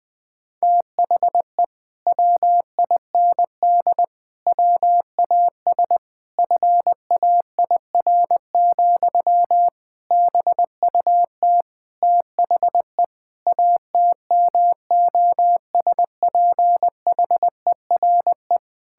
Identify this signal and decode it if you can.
Morse code, heard as a keyed tone, about 20 wpm, 700 Hz. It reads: THE WIND WAS FAIR, BUT THE ATMOSPHERE
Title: Sentences from Top 1600 Words 20wpm